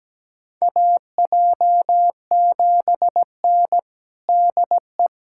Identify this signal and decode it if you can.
Morse code, heard as a keyed tone, about 17 wpm, 700 Hz. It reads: AJ7N DE
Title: USA Callsign DE Callsign - 4 Characters - 17wpm